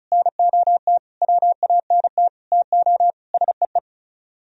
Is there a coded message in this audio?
Yes: Morse code from 0.1 to 3.8 s, heard as a keyed tone, about 38 wpm, 700 Hz. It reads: NOT WANT TO SEE